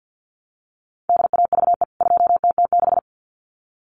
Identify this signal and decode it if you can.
Morse code, heard as a keyed tone, about 50 wpm, 700 Hz. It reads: DAVE 2TT6
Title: CWT Exchange 50wpm